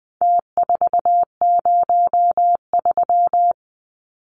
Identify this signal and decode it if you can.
Morse code, heard as a keyed tone, about 20 wpm, 700 Hz. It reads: T403
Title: Numbers - Contest Style - 4-Digit 20wpm